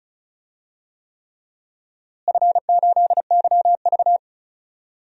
Morse code, heard as a keyed tone, about 36 wpm, 700 Hz. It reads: F8YV
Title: DXCC-100 Call Signs - 4 Characters - Repeated 3x - 35wpm